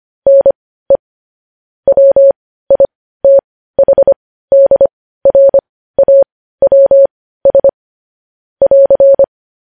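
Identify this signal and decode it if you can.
Morse code, heard as a keyed tone, about 25 wpm, spaced as for 15 wpm, 550 Hz. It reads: NE WITHDRAWS <AR>
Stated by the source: Morse code news headlines at 15 wpm.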